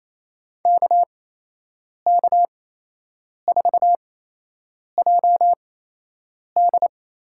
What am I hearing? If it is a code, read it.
Morse code, heard as a keyed tone, about 28 wpm, spaced as for 8 wpm, 700 Hz. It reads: KK4JD
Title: USA Call Signs - 8x Character Spacing - 28wpm